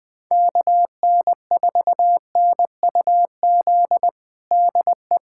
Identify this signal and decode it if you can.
Morse code, heard as a keyed tone, about 20 wpm, 700 Hz. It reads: KN4NUZ DE
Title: USA Callsign DE Callsign 20wpm